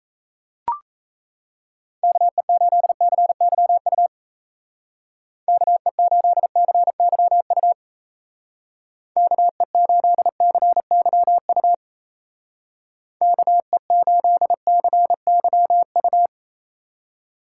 Morse code, heard as a keyed tone, about 35 wpm, 700 Hz. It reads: KE8CYU KE8CYU KE8CYU KE8CYU